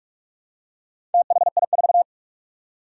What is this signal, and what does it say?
Morse code, heard as a keyed tone, about 44 wpm, 700 Hz. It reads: THI4
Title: Four Challenging Letter-Number Combinations 45wpm